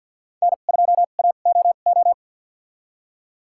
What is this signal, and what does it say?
Morse code, heard as a keyed tone, about 49 wpm, spaced as for 35 wpm, 700 Hz. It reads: N2AOO
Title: USA Call Signs - 5 Characters - 2x Character Spacing - 50wpm